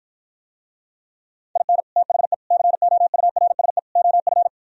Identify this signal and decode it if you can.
Morse code, heard as a keyed tone, about 58 wpm, 700 Hz. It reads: IN THE COURSE OF